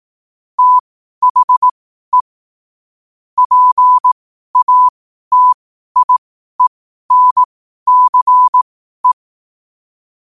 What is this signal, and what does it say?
Morse code, heard as a keyed tone, about 18 wpm, spaced as for 12 wpm, 1000 Hz. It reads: THE PATIENCE
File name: qotdfiles / 12wpm